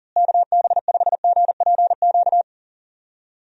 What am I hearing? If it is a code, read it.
Morse code, heard as a keyed tone, about 40 wpm, 700 Hz. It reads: KB5GPQ
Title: USA Call Signs - 6 Characters - Repeated 3x - 40wpm